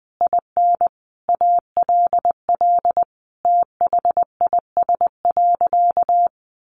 Morse code, heard as a keyed tone, about 20 wpm, 700 Hz. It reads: IN ALL THIS.
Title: Sentences from Top 1400 Words 20wpm